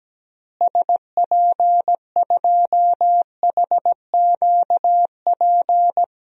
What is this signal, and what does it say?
Morse code, heard as a keyed tone, about 17 wpm, 700 Hz. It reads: SP2HQP
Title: DXCC-100 Call Signs - 6 Characters - Repeated 3x - 17wpm